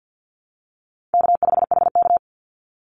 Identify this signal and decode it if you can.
Morse code, heard as a keyed tone, about 50 wpm, 700 Hz. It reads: K5HK
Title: Easy Call Signs Repeated 2x - 50wpm